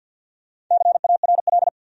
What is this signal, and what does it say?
Morse code, heard as a keyed tone, about 53 wpm, 700 Hz. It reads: CARL
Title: USA Common Names - Max 5 Letters - 50wpm